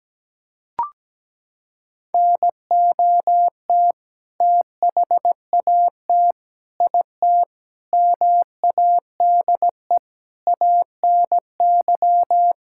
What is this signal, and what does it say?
Morse code, heard as a keyed tone, about 17 wpm, 700 Hz. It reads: NOT THAT IT MADE ANY
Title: Sentences from Top 1000 Words - Review Part 5 - 17wpm